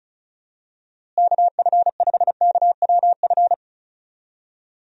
Morse code, heard as a keyed tone, about 34 wpm, 700 Hz. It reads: KF5KWF
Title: USA Call Signs - 6 Characters - Repeated 2x - 35wpm